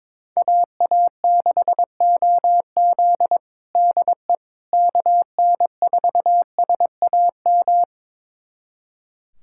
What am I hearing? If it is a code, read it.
Morse code, heard as a keyed tone, about 21 wpm, 700 Hz. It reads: AA6OZ DE KN4SAM
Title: USA Callsign DE Callsign 22wpm